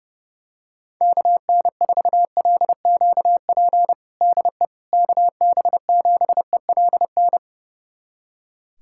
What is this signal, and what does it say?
Morse code, heard as a keyed tone, about 30 wpm, 700 Hz. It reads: KN4LQP DE KB7ELN
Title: USA Callsign DE Callsign 30wpm